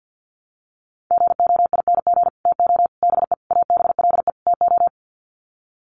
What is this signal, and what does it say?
Morse code, heard as a keyed tone, about 50 wpm, 700 Hz. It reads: GOING TO BE ABLE TO
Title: Sets of 5 Words 50wpm